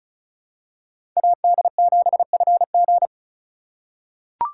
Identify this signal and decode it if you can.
Morse code, heard as a keyed tone, about 36 wpm, 700 Hz. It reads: AD7FG
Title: USA Call Signs - 5 Characters - 4x Character Spacing - 35wpm